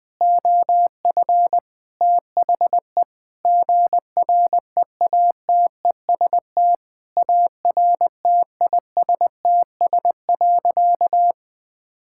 Morse code, heard as a keyed tone, about 21 wpm, 700 Hz. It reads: OF THE GREATEST ARTISTS.
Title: Sentences with 8-Word Phrases - 20wpm